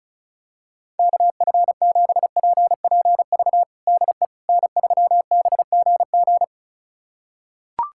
Morse code, heard as a keyed tone, about 36 wpm, 700 Hz. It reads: KF7PPV DE N3BGG
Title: USA Callsign DE Callsign 35wpm